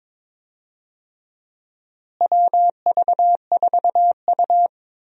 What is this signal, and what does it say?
Morse code, heard as a keyed tone, about 21 wpm, 700 Hz. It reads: WV4U
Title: USA Call Signs - 4 Characters - Repeated 2x - 22wpm